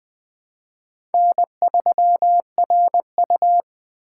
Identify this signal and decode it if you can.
Morse code, heard as a keyed tone, about 20 wpm, 700 Hz. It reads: N3RU
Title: USA Call Signs - 4 Characters - 8x Character Spacing - 20wpm